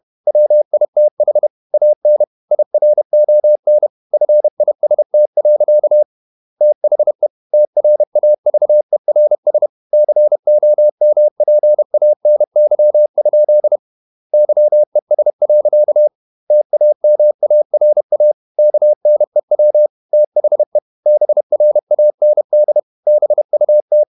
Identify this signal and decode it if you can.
Morse code, heard as a keyed tone, about 31 wpm, 600 Hz. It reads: WITH AN IRON FIST. THE TRAVERS COMPANY? YES. TAMARA KNEW THE BRAND BUT
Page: Never 31wpm | CW med Gnister